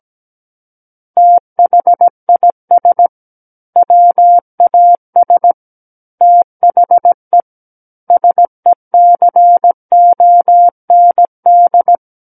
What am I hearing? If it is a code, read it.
Morse code, heard as a keyed tone, about 17 wpm, 700 Hz. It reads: THIS WAS THE SECOND
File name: Book One / 08-Chapter 8 - 17 WPM